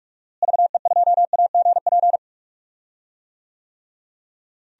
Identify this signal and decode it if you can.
Morse code, heard as a keyed tone, about 43 wpm, 700 Hz. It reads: VE2AGP
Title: DXCC-100 Call Signs - 6 Characters - Repeated 2x - 45wpm